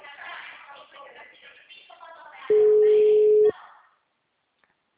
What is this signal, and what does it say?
Ringback tone (Europe, 425 Hz), with speech.